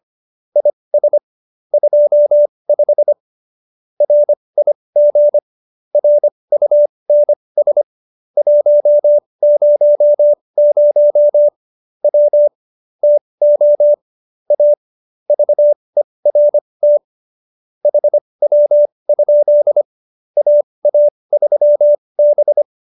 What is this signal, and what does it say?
Morse code, heard as a keyed tone, about 25 wpm, spaced as for 20 wpm, 590 Hz. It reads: IS 25 RIG RUNS 100 W TO A VERT HW? AA3B
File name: QSO302 20